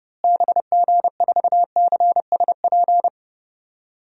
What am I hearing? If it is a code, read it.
Morse code, heard as a keyed tone, about 30 wpm, 700 Hz. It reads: BG4CSP
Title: DXCC-100 Call Signs - 6 Characters - Repeated 3x - 30wpm